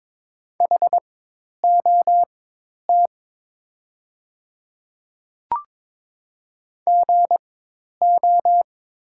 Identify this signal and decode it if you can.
Morse code, heard as a keyed tone, about 22 wpm, spaced as for 10 wpm, 700 Hz. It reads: HOT GO
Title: 4x Character Spacing - Top 100 Words - 22wpm